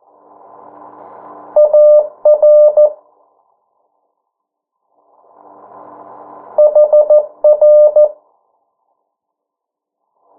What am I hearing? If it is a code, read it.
Morse code, heard as a keyed tone, about 14 wpm, 600 Hz. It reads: AR HR
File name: morse mp3 / 09 urok 8